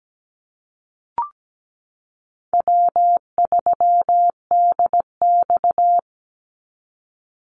Morse code, heard as a keyed tone, about 17 wpm, 700 Hz. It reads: W3DX
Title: NA SOTA Activator Callsigns 17wpm